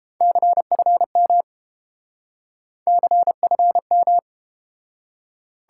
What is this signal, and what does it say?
Morse code, heard as a keyed tone, about 30 wpm, 700 Hz. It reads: CFM CFM